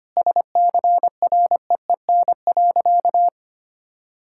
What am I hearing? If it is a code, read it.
Morse code, heard as a keyed tone, about 25 wpm, 700 Hz. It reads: SCREEN.
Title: Sentences from Top 1000 Words - Review Part 6 - 25wpm